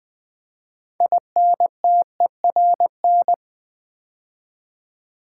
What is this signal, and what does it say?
Morse code, heard as a keyed tone, about 20 wpm, 700 Hz. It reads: INTERN
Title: Build A Word - 2 Letters At A Time - 8-Letter Words - 20wpm